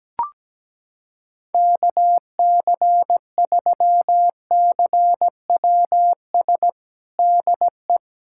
Morse code, heard as a keyed tone, about 17 wpm, 700 Hz. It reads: KC3CWS DE
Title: USA Callsign DE Callsign - 6 Characters - 17wpm